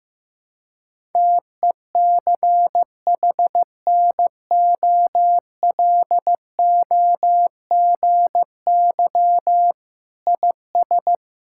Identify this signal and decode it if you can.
Morse code, heard as a keyed tone, about 15 wpm, 700 Hz. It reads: TECHNOLOGY IS
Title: Sentences from Top 1000 Words - Review Part 5 - 15wpm